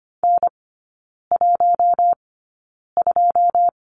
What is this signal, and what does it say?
Morse code, heard as a keyed tone, about 25 wpm, spaced as for 9 wpm, 690 Hz. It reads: N12
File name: Lesson 14e - Number 2 - Three Characters 25wpm